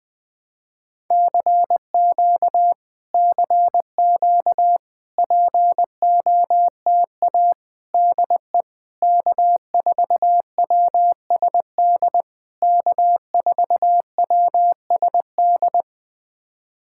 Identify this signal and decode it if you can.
Morse code, heard as a keyed tone, about 20 wpm, 700 Hz. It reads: CQ CQ POTA DE K4WSD K4WSD
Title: POTA QSO Practice - 20wpm